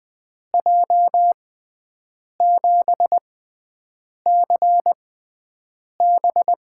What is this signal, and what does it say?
Morse code, heard as a keyed tone, about 20 wpm, spaced as for 7 wpm, 700 Hz. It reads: J7CB
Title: USA Call Signs - 6x Character Spacing - 20wpm